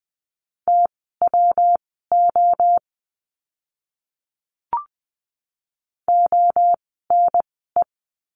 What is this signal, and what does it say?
Morse code, heard as a keyed tone, about 20 wpm, spaced as for 15 wpm, 700 Hz. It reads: TWO ONE
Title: 115 Lesson 19f - Letter w - Words 20wpm